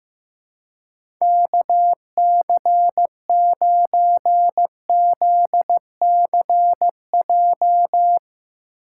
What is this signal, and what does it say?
Morse code, heard as a keyed tone, about 15 wpm, 700 Hz. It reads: KC9ZCJ
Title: USA Call Signs - 6 Characters - 4x Character Spacing - 15wpm